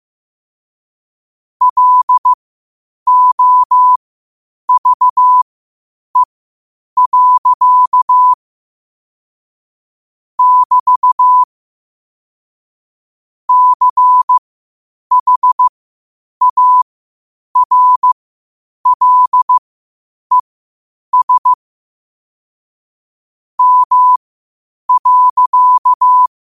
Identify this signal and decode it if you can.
Morse code, heard as a keyed tone, about 15 wpm, spaced as for 8 wpm, 1000 Hz. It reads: LOVE. <BT> CHARLES M.
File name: qotdfiles / 8wpm